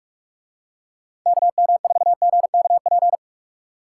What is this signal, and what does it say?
Morse code, heard as a keyed tone, about 46 wpm, 700 Hz. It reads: KM4GKP